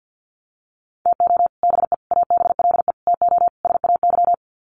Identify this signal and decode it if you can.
Morse code, heard as a keyed tone, about 48 wpm, 700 Hz. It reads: TO BE ABLE TO SAY
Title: Sets of 5 Words Encore2 50wpm